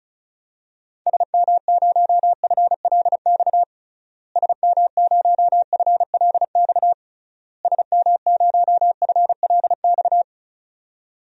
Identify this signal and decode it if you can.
Morse code, heard as a keyed tone, about 36 wpm, 700 Hz. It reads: SM0FLX SM0FLX SM0FLX
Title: DXCC-100 Call Signs - 6 Characters - Repeated 3x - 35wpm